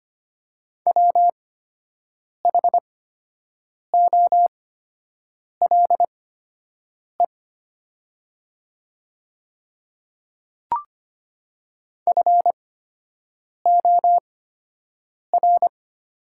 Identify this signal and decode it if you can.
Morse code, heard as a keyed tone, about 25 wpm, spaced as for 7 wpm, 700 Hz. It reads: WHOLE FOR
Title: 8x Character Spacing - Top 500 Words - 25wpm